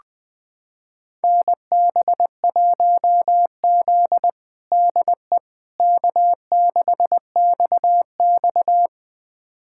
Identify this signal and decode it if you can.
Morse code, heard as a keyed tone, about 21 wpm, 700 Hz. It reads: NB1Z DE K6XX
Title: USA Callsign DE Callsign - 4 Characters - 20wpm